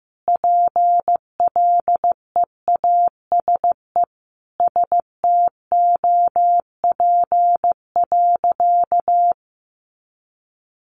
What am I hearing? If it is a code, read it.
Morse code, heard as a keyed tone, about 15 wpm, 700 Hz. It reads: PLEASE STOP.
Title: Sentences from Top 1500 Words 15wpm